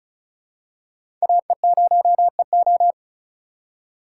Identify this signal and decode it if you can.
Morse code, heard as a keyed tone, about 34 wpm, 700 Hz. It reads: AE0EO
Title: USA Call Signs - 5 Characters - 3x Character Spacing - 35wpm